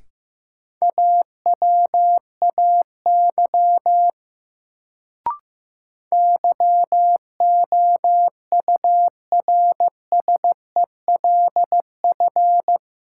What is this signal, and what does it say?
Morse code, heard as a keyed tone, about 15 wpm, 700 Hz. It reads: AWAY YOURSELF
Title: Sentences from Top 600 Words - New Word Drill - 15wpm